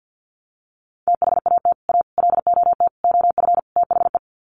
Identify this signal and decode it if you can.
Morse code, heard as a keyed tone, about 48 wpm, 700 Hz. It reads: THAT A LOT OF THE
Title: Sets of 5 Words Encore2 50wpm